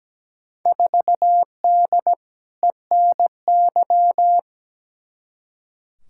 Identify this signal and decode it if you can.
Morse code, heard as a keyed tone, about 17 wpm, 700 Hz. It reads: 4D ENY
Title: Field Day Exchanges Encore - 17wpm